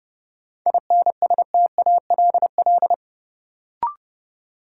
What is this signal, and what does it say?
Morse code, heard as a keyed tone, about 30 wpm, 700 Hz. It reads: INSTALL
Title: Sentences from Top 1600 Words - New Word Drill - 30wpm